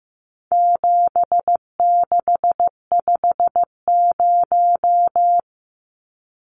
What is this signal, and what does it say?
Morse code, heard as a keyed tone, about 15 wpm, 700 Hz. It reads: 7650
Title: Numbers - 4-Digit 15wpm